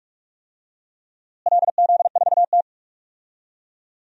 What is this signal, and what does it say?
Morse code, heard as a keyed tone, about 46 wpm, 700 Hz. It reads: LZ4T
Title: DXCC-100 Call Signs - 4 Characters - Repeated 3x - 45wpm